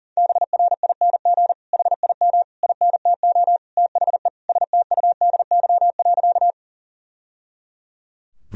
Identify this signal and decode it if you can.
Morse code, heard as a keyed tone, about 37 wpm, 700 Hz. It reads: BRING HIM INTO THE STUDY.
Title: Sentences from Top 1000 Words - Review Part 1 - 40wpm